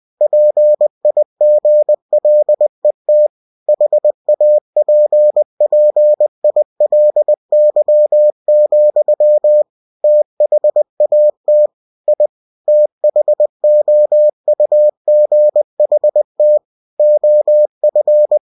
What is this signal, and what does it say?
Morse code, heard as a keyed tone, about 20 wpm, 600 Hz. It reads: PIGLET HAPPILY, THAT I THOUGHT OF
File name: cw-books / winnie-the-pooh / winnie-20wpm-page-0058